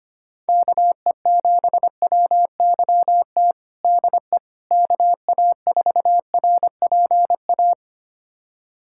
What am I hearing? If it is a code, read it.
Morse code, heard as a keyed tone, about 25 wpm, 700 Hz. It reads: KE7WYT DE KA4RPA
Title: USA Callsign DE Callsign 25wpm